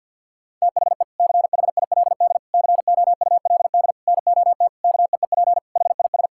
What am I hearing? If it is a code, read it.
Morse code, heard as a keyed tone, about 50 wpm, 700 Hz. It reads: THE CHILD COULD NOT KEEP HIS
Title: Sentences from Top 1000 Words - Review Part 6 - 50wpm